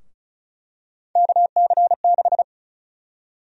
Morse code, heard as a keyed tone, about 36 wpm, 700 Hz. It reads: KC6
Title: USA Call Sign Prefixes - 35wpm